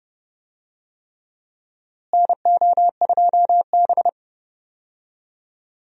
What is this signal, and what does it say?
Morse code, heard as a keyed tone, about 30 wpm, 700 Hz. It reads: NO2B
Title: USA Call Signs - 4 Characters - Repeated 3x - 30wpm